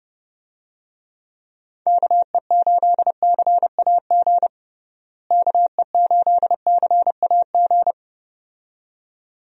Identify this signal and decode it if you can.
Morse code, heard as a keyed tone, about 29 wpm, 700 Hz. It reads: KE8CAG KE8CAG
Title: USA Call Signs - 6 Characters - Repeated 2x - 30wpm